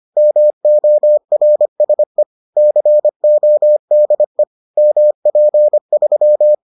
Morse code, heard as a keyed tone, about 25 wpm, 600 Hz. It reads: MORSE CODE MP3